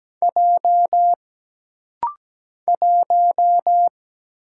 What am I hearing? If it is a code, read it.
Morse code, heard as a keyed tone, about 17 wpm, 700 Hz. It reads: J 1
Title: Single Letter-Number - Rapid-Fire 17wpm